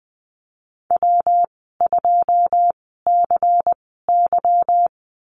Morse code, heard as a keyed tone, about 19 wpm, spaced as for 14 wpm, 700 Hz. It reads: W2CY
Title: 167 Lesson 27g - Number 7 - Call Signs 20wpm